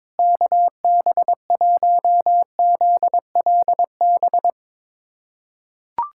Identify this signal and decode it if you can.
Morse code, heard as a keyed tone, about 22 wpm, 700 Hz. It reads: KB1ZLB
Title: USA Callsign DE Callsign 22wpm